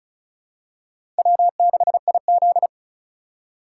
Morse code, heard as a keyed tone, about 34 wpm, 700 Hz. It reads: W6IZ
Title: USA Call Signs - 8x Character Spacing - 35wpm